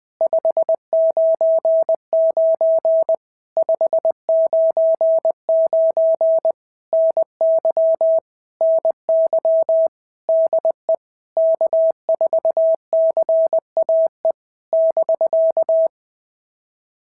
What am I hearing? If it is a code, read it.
Morse code, heard as a keyed tone, about 20 wpm, 650 Hz. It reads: 599 599 NY NY DE K4CAE <BK>
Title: POTA QSO Practice - 20wpm